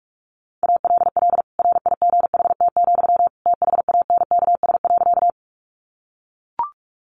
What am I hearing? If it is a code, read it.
Morse code, heard as a keyed tone, about 44 wpm, 700 Hz. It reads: ALL RIGHT, THANKS.
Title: Sentences from Top 600 Words 45wpm